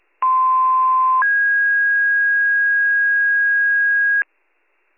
Two-tone page, heard as a one-second tone, then a three-second tone, about 1040 then 1650 Hz, with noise. It is heard from 0.2 to 4.2 s.